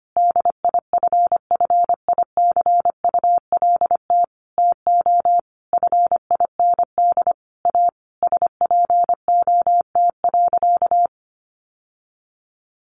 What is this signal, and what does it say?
Morse code, heard as a keyed tone, about 25 wpm, 700 Hz. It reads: DIFFICULT TO FIND A SPOT.
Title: Sentences from Top 1100 Words 25wpm